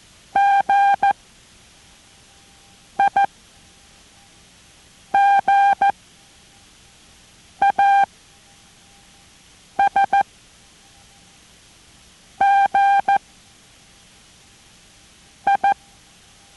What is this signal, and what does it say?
Morse code, heard as a keyed tone, about 14 wpm, 800 Hz. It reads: G I G A S G I